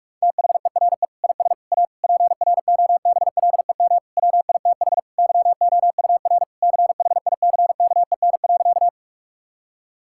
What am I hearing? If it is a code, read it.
Morse code, heard as a keyed tone, about 46 wpm, 700 Hz. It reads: THERE IS A PROBLEM WITH YOUR CHICKEN.
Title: Sentences from Top 1000 Words - Review Part 5 - 45wpm